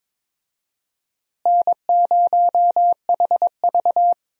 Morse code, heard as a keyed tone, about 22 wpm, 700 Hz. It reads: N0HV
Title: USA Call Signs - 4 Characters - Repeated 3x - 22wpm